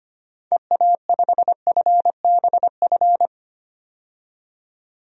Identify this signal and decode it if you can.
Morse code, heard as a keyed tone, about 25 wpm, 700 Hz. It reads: EA5FBF
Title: DXCC-100 Call Signs - 6 Characters - 25wpm